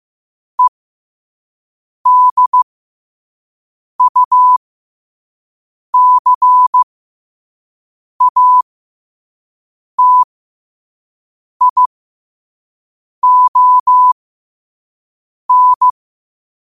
Morse code, heard as a keyed tone, about 15 wpm, spaced as for 5 wpm, 1000 Hz. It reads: EDUCATION
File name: qotdfiles / 5wpm